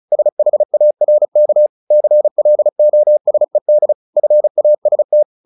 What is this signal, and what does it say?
Morse code, heard as a keyed tone, about 34 wpm, 600 Hz. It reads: SHARK CLOSED FAST
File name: the old man and the sea 2 35wpm